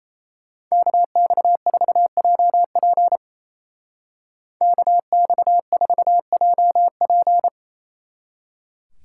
Morse code, heard as a keyed tone, about 33 wpm, 700 Hz. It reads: KX4JP KX4JP
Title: Speed Racing - USA Call Signs 28wpm